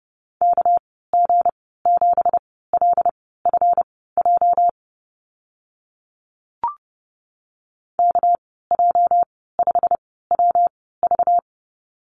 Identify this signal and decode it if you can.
Morse code, heard as a keyed tone, about 30 wpm, spaced as for 17 wpm, 700 Hz. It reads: KG7LFJ KJ5WV
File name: Lesson 37g - Letter x - Call Signs 30wpm